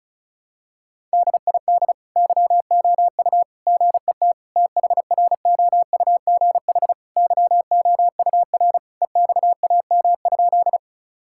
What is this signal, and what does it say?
Morse code, heard as a keyed tone, about 34 wpm, 700 Hz. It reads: DID YOU GET THROUGH YOUR EXAM?
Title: Sentences from Top 1000 Words - Review Part 5 - 35wpm